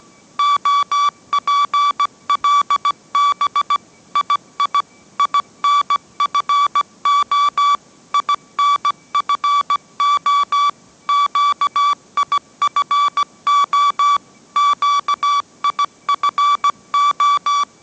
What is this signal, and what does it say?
Morse code, heard as a keyed tone, about 17 wpm, 1150 Hz. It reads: OPLB II INFO INFO QIFO QIFO
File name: mrx / mrx080 18wpm